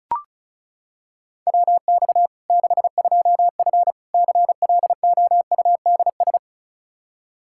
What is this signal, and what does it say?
Morse code, heard as a keyed tone, about 36 wpm, 700 Hz. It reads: WX 62F CLOUDS
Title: K1USN SST QSO Practice 35wpm